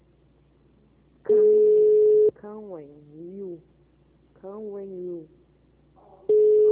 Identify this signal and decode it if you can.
Ringback tone (Europe, 425 Hz), with speech.
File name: out- -2005- - - .425